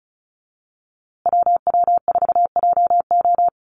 Morse code, heard as a keyed tone, about 34 wpm, 700 Hz. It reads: WW4JO
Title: USA Call Signs Repeated 2x 35wpm